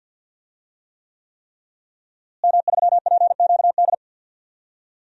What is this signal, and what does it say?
Morse code, heard as a keyed tone, about 52 wpm, 700 Hz. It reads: M3PQD
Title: DXCC-100 Call Signs - 5 Characters - Repeated 3x - 50wpm